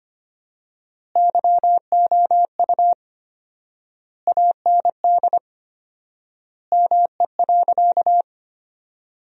Morse code, heard as a keyed tone, about 25 wpm, 700 Hz. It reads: YOU AND ME.
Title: 4x Word-Spacing - Sentences from Top 500 Words - 25wpm